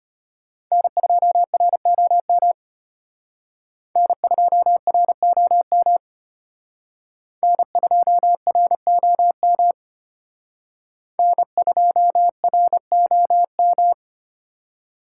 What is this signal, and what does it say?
Morse code, heard as a keyed tone, about 32 wpm, 700 Hz. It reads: N2ROM N2ROM N2ROM N2ROM